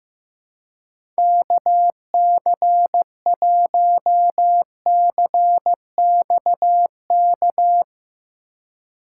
Morse code, heard as a keyed tone, about 15 wpm, 700 Hz. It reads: KC1CXK
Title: USA Call Signs - 6 Characters - 3x Character Spacing - 15wpm